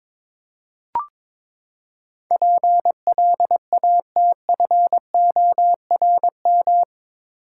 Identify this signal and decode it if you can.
Morse code, heard as a keyed tone, about 22 wpm, 700 Hz. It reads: PLATFORM
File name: practice / 22wpm / Eight Letter Words - 22wpm